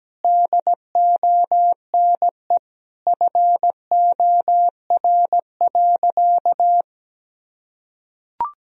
Morse code, heard as a keyed tone, about 17 wpm, 700 Hz. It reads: DONE FOR.
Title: Sentences from Top 1000 Words - Review Part 4 - 17wpm